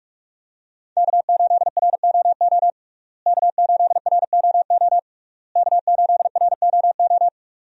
Morse code, heard as a keyed tone, about 46 wpm, 700 Hz. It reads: K8ROO K8ROO K8ROO
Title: USA Call Signs - 5 Characters - Repeated 3x - 45wpm